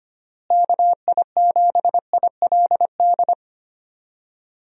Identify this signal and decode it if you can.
Morse code, heard as a keyed tone, about 25 wpm, 700 Hz. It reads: KI7ILD
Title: USA Call Signs - 6 Characters - Repeated 3x - 25wpm